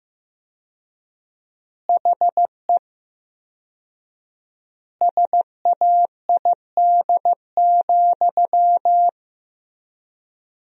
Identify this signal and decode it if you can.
Morse code, heard as a keyed tone, about 14 wpm, 700 Hz. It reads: HE SAID,
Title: 4x Word-Spacing - Sentences from Top 500 Words - 15wpm